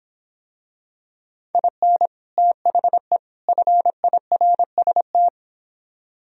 Morse code, heard as a keyed tone, about 25 wpm, 700 Hz. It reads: IN THE FIRST